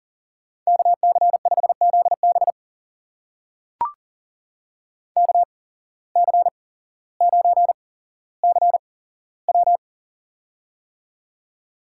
Morse code, heard as a keyed tone, about 40 wpm, spaced as for 17 wpm, 700 Hz. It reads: KC5ZB KC9CW
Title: USA Call Signs - 5 Characters - 8x Character Spacing - 40wpm